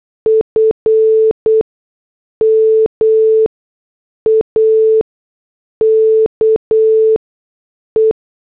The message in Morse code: F MAKE